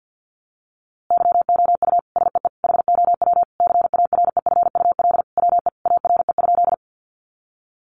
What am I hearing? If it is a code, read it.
Morse code, heard as a keyed tone, about 50 wpm, 700 Hz. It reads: YOU SEE HOW CAREFUL WE ARE?
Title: A Sound Of Thunder 50wpm